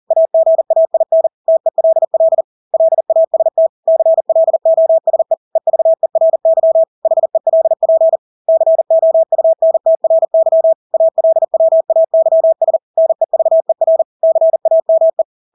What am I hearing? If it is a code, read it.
Morse code, heard as a keyed tone, about 40 wpm, 650 Hz. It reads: AGAIN TELL LAST CLOSE EVERY HELP COUNTRY ALWAYS NEVER CAME
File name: Engelske ord 40-40 wpm